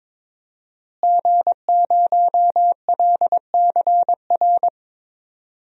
Morse code, heard as a keyed tone, about 21 wpm, 700 Hz. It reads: G0LCR
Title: DXCC-100 Call Signs - 5 Characters - Repeated 2x - 22wpm